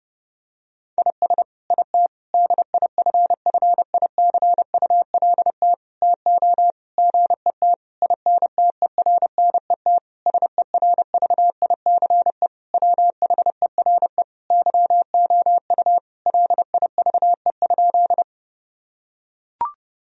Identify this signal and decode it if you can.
Morse code, heard as a keyed tone, about 30 wpm, 700 Hz. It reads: IS IT DIFFICULT TO GET INTERNET SERVICE WHERE YOU LIVE?
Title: Sentences from Top 1000 Words - Review Part 4 - 30wpm